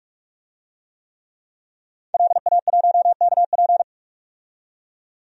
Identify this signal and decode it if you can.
Morse code, heard as a keyed tone, about 47 wpm, 700 Hz. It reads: LA1KP
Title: DXCC-100 Call Signs - 5 Characters - 45wpm